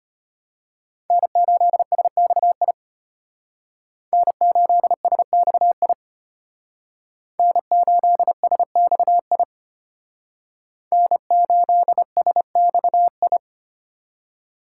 Morse code, heard as a keyed tone, about 32 wpm, 700 Hz. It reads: N8SXI N8SXI N8SXI N8SXI